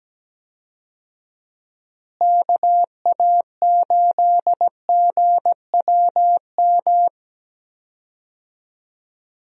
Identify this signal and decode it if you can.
Morse code, heard as a keyed tone, about 17 wpm, 700 Hz. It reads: KA8GWM
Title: USA Call Signs - 6 Characters - Repeated 2x - 17wpm